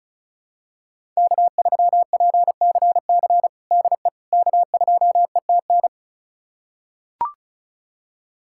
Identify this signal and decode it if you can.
Morse code, heard as a keyed tone, about 34 wpm, 700 Hz. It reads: K3PCC DE K2ETN
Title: USA Callsign DE Callsign - 5 Characters - 35wpm